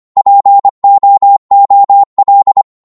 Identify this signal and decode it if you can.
Morse code, heard as a keyed tone, about 25 wpm, 800 Hz. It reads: POOL